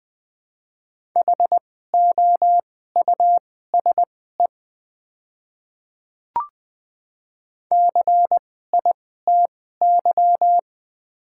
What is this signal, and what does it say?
Morse code, heard as a keyed tone, about 20 wpm, spaced as for 14 wpm, 700 Hz. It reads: HOUSE CITY
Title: 2x Character Spacing - Top 300 Words - 20wpm